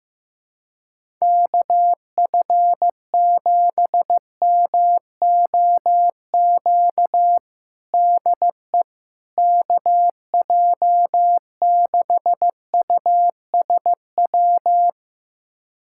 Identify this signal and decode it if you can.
Morse code, heard as a keyed tone, about 15 wpm, 700 Hz. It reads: KF7MOQ DE KJ6USW
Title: USA Callsign DE Callsign - 6 Characters - 15wpm